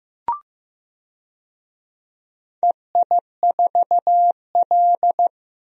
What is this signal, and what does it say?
Morse code, heard as a keyed tone, about 15 wpm, 700 Hz. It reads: EI4L
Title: DXCC-100 Call Signs - 4 Characters - 15wpm